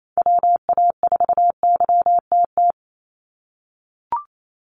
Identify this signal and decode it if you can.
Morse code, heard as a keyed tone, about 28 wpm, 700 Hz. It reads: WA4YTT
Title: USA Call Signs 28wpm